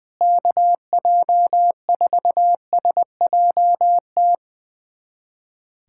Morse code, heard as a keyed tone, about 21 wpm, 700 Hz. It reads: KJ4SJT
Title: USA Callsign DE Callsign 20wpm